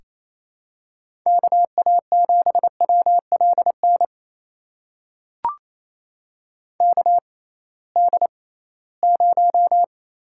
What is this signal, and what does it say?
Morse code, heard as a keyed tone, about 27 wpm, spaced as for 18 wpm, 700 Hz. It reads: KA7WLN KD0
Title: USA Call Signs - 6x Character Spacing - 28wpm